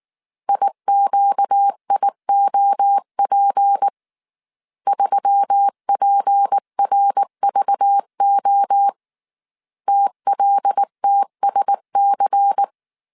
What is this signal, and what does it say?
Morse code, heard as a keyed tone, about 19 wpm, 800 Hz. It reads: IQIOP 3PRVO TLTSC
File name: CW / DET / 1 6 1500C S C 18wpm 2356